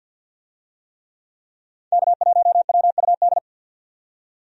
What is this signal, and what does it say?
Morse code, heard as a keyed tone, about 51 wpm, 700 Hz. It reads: K1WUD
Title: USA Call Signs - 5 Characters - 50wpm